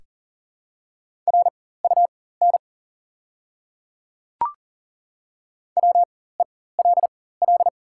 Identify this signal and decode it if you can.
Morse code, heard as a keyed tone, about 41 wpm, spaced as for 19 wpm, 690 Hz. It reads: RUN WELL
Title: 4x Character Spacing - Top 500 Words - 40wpm